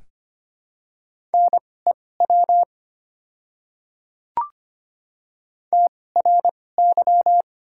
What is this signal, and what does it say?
Morse code, heard as a keyed tone, about 25 wpm, spaced as for 18 wpm, 700 Hz. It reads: NEW TRY
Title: 2x Character Spacing - Top 200 Words - 25wpm